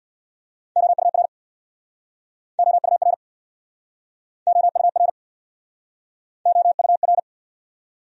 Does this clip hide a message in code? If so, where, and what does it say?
Yes: Morse code from 0.8 to 7.2 s, heard as a keyed tone, about 67 wpm, 700 Hz. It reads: OUR OUR OUR OUR